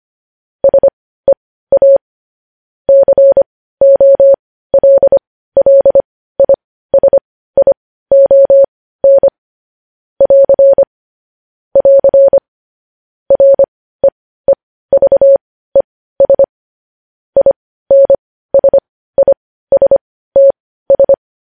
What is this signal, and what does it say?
Morse code, heard as a keyed tone, about 25 wpm, spaced as for 15 wpm, 550 Hz. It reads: SEA COLLISION <AR> <AR> REEVES INSISTS
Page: news headlines in morse code at 15 wpm